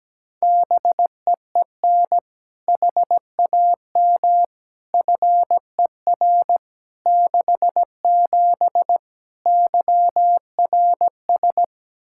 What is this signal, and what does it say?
Morse code, heard as a keyed tone, about 17 wpm, 700 Hz. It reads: BEEN HAM FER 67 YRS
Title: Practice QSO - Exchange 3 - More Info - 17wpm